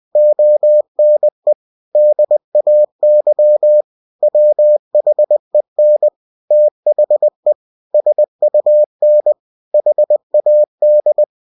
Morse code, heard as a keyed tone, about 20 wpm, 600 Hz. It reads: ONE DAY WHEN THE SUN HAD
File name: cw-books / winnie-the-pooh / winnie-20wpm-page-0101